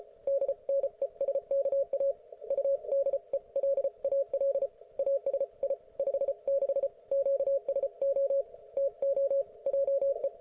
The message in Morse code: DNESKA UDELAL ASI 56 QSO TO JE